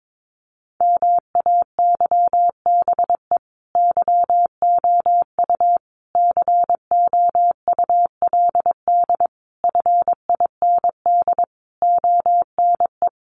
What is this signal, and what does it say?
Morse code, heard as a keyed tone, about 22 wpm, 700 Hz. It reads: MAYBE YOU COULD FIND ONE
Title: Sentences from Top 1400 Words 22wpm